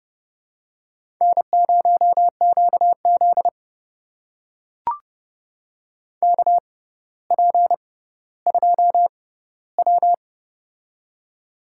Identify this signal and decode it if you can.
Morse code, heard as a keyed tone, about 30 wpm, spaced as for 15 wpm, 700 Hz. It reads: N0QZ KP2W
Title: USA Call Signs - 4 Characters - 6x Character Spacing - 30wpm